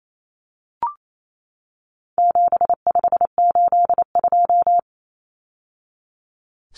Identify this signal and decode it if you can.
Morse code, heard as a keyed tone, about 28 wpm, 700 Hz. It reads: 7582
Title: Numbers - 4-Digit 28wpm